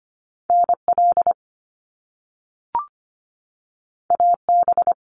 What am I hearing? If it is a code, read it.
Morse code, heard as a keyed tone, about 25 wpm, 700 Hz. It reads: NL AB
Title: Canadian Provinces and Territories 25wpm